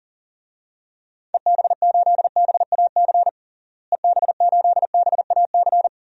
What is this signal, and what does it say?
Morse code, heard as a keyed tone, about 37 wpm, 700 Hz. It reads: EB8BAC EB8BAC
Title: DXCC-100 Call Signs - 6 Characters - Repeated 2x - 40wpm